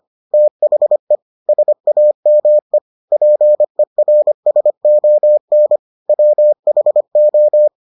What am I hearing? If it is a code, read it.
Morse code, heard as a keyed tone, about 25 wpm, 600 Hz. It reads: THE SAME PERSON WHO